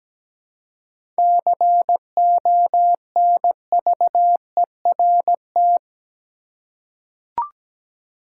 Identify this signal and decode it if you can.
Morse code, heard as a keyed tone, about 17 wpm, 700 Hz. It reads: CONVERT
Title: Seven Letter Words - 17wpm